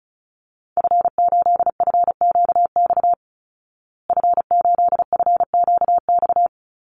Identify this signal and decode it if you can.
Morse code, heard as a keyed tone, about 34 wpm, 700 Hz. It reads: F8FQX F8FQX
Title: Call Signs Repeated 3x - 35wpm